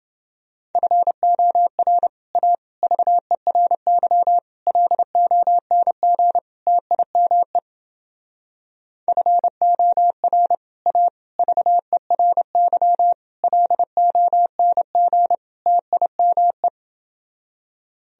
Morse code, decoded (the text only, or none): FOR A VERY LONG TIME FOR A VERY LONG TIME